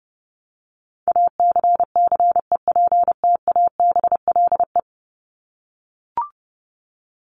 Morse code, heard as a keyed tone, about 30 wpm, 700 Hz. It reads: ACCEPTABLE
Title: Top 1501-1750 Words 30wpm